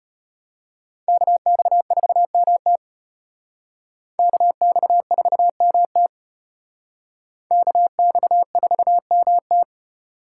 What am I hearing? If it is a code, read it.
Morse code, heard as a keyed tone, about 32 wpm, 700 Hz. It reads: KX4MT KX4MT KX4MT